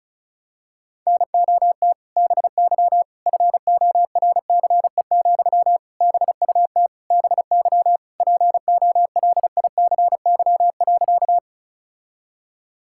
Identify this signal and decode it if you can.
Morse code, heard as a keyed tone, about 34 wpm, 700 Hz. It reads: NOT BY FORCE, BUT BY POLICY.
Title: Sentences from Top 1000 Words - Review Part 5 - 35wpm